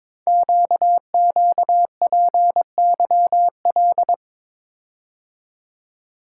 Morse code, heard as a keyed tone, about 23 wpm, 700 Hz. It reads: QQPYL
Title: Code Groups - 5 Letters - 22wpm